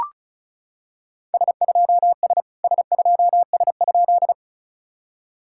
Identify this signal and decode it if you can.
Morse code, heard as a keyed tone, about 36 wpm, 700 Hz. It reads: S2S S2S?
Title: SOTA QSO Practice 35wpm